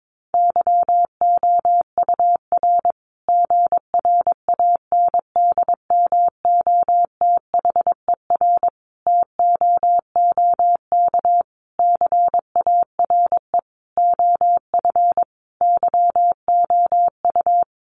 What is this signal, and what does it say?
Morse code, heard as a keyed tone, about 22 wpm, 700 Hz. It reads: YOUR GRANDMOTHER TOOK CARE OF YOU
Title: Sentences from Top 1200 Words 22wpm